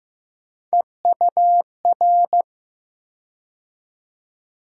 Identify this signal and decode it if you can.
Morse code, heard as a keyed tone, about 15 wpm, 700 Hz. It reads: EUR
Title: USA Call Sign Postfixes - 3 Letters 15wpm